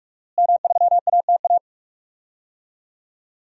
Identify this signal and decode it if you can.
Morse code, heard as a keyed tone, about 46 wpm, 700 Hz. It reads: M3ATA
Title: DXCC-100 Call Signs - 5 Characters - Repeated 3x - 45wpm